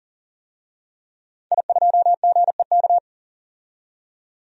Morse code, heard as a keyed tone, about 40 wpm, 700 Hz. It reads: I2GEK
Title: DXCC-100 Call Signs - 5 Characters - 40wpm